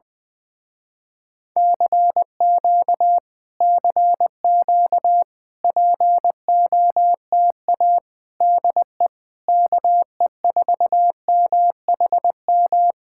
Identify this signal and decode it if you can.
Morse code, heard as a keyed tone, about 20 wpm, 700 Hz. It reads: CQ CQ POTA DE KE4MHM